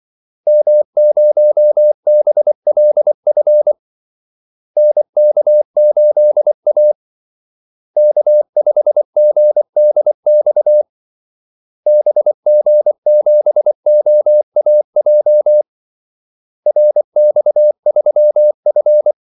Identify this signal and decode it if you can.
Morse code, heard as a keyed tone, about 24 wpm, 600 Hz. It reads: M0BLF NK8A K5GDX BG7OAJ RX3F
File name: Callsigns-24wpm